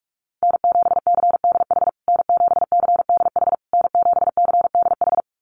Morse code, heard as a keyed tone, about 44 wpm, 690 Hz. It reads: N7CDH N7CDH N7CDH
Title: USA Call Signs Repeated 3x 45wpm